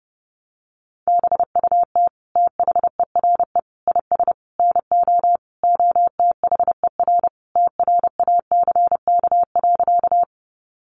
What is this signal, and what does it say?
Morse code, heard as a keyed tone, about 30 wpm, 700 Hz. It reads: BUT THERE IS NO OTHER TRACK.
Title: Sentences from Top 1100 Words 30wpm